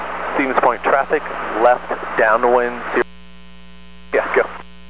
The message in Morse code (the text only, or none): none